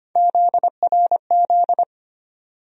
Morse code, heard as a keyed tone, about 25 wpm, 700 Hz. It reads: ZRZ
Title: USA Call Sign Postfixes - 3 Letters 25wpm